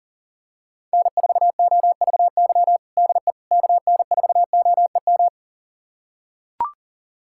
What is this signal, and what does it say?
Morse code, heard as a keyed tone, about 40 wpm, 700 Hz. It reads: N4OVY DE KN4OEM
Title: USA Callsign DE Callsign 40wpm